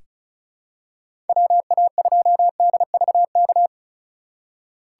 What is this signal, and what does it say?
Morse code, heard as a keyed tone, about 34 wpm, 700 Hz. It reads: WA2DVK
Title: USA Call Signs - 6 Characters - 2x Character Spacing - 35wpm